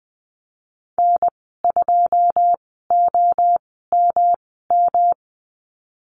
Morse code, heard as a keyed tone, about 20 wpm, spaced as for 15 wpm, 700 Hz. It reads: N2OMM
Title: 128 Lesson 21g - Number 6 - Call Signs 20wpm